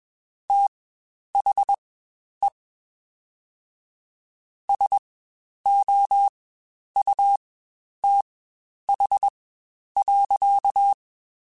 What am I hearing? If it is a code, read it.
Morse code, heard as a keyed tone, about 21 wpm, spaced as for 10 wpm, 800 Hz. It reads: THE SOUTH.
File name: Spaced Characters / Spaced Characters-mp3s / Weather 4 21 sp @20